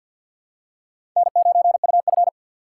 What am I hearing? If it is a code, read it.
Morse code, heard as a keyed tone, about 52 wpm, 700 Hz. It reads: N9UF
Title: USA Call Signs - 4x Character Spacing - 50wpm